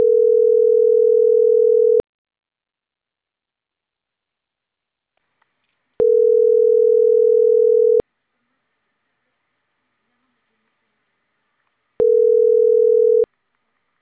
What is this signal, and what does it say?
Ringback tone (North America), 440 + 480 Hz.